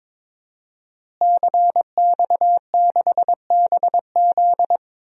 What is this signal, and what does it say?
Morse code, heard as a keyed tone, about 22 wpm, 700 Hz. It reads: CX6BZ
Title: DXCC-100 Call Signs - 5 Characters - Repeated 2x - 22wpm